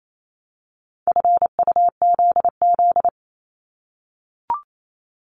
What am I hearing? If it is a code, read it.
Morse code, heard as a keyed tone, about 28 wpm, 700 Hz. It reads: FUZZ
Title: Multi-letter Phonemes 28wpm